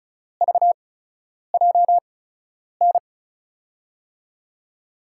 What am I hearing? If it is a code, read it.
Morse code, heard as a keyed tone, about 36 wpm, spaced as for 10 wpm, 700 Hz. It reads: VJN
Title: USA Call Signs - 6 Characters - 8x Character Spacing - 35wpm